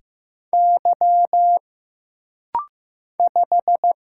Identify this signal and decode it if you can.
Morse code, heard as a keyed tone, about 15 wpm, 700 Hz. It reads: Y 5
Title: Single Letter-Number Alt - Rapid-Fire 15wpm